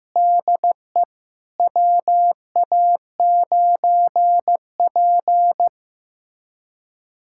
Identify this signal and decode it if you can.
Morse code, heard as a keyed tone, about 15 wpm, 700 Hz. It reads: DE WA9P
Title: USA Callsign DE Callsign - 4 Characters - 15wpm